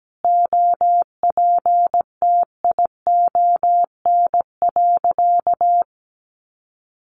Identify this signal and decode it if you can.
Morse code, heard as a keyed tone, about 17 wpm, 700 Hz. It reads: OPTION.
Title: Sentences from Top 1300 Words 17wpm